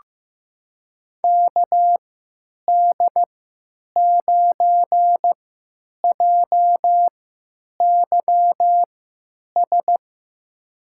Morse code, heard as a keyed tone, about 15 wpm, spaced as for 9 wpm, 700 Hz. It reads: KD9JYS
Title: USA Call Signs - 6 Characters - 3x Character Spacing - 15wpm